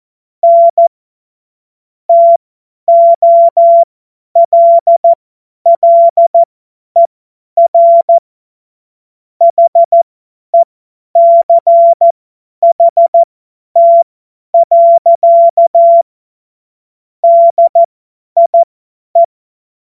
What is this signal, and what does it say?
Morse code, heard as a keyed tone, about 14 wpm, spaced as for 10 wpm, 680 Hz. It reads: N TOLLER HECHT. DIE
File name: Morsekurs / Witze / Witz 30. August 2022 50 F70